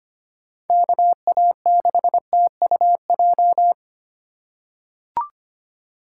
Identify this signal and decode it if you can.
Morse code, heard as a keyed tone, about 25 wpm, 700 Hz. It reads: KA6TUJ
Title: USA Call Signs - 6 Characters - 2x Character Spacing - 25wpm